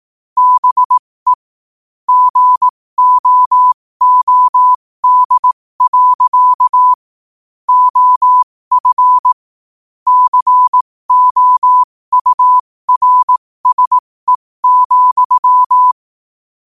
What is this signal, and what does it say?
Morse code, heard as a keyed tone, about 18 wpm, spaced as for 15 wpm, 1000 Hz. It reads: BE GOOD. OF COURSE,
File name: qotdfiles / 15wpm